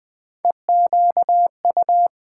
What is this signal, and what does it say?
Morse code, heard as a keyed tone, about 20 wpm, 700 Hz. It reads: EQU
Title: Build A Word - 1 Letter At A Time - 5-Letter Words - 20wpm